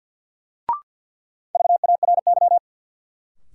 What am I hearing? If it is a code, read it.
Morse code, heard as a keyed tone, about 52 wpm, 700 Hz. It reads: VARY
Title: Sentences from Top 1200 Words - New Word Drill Alt - 50wpm